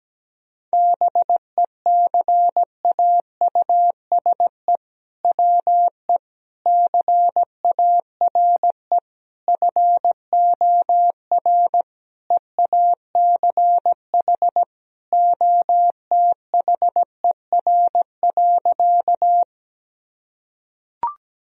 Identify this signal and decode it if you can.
Morse code, heard as a keyed tone, about 17 wpm, 700 Hz. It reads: BECAUSE WE CARE FOR EACH OTHER.
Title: Sentences from Top 1000 Words - Review Part 3 - 17wpm